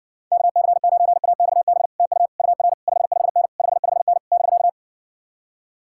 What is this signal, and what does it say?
Morse code, heard as a keyed tone, about 60 wpm, 700 Hz. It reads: KC9AXB TU UR 55N 55N <BK>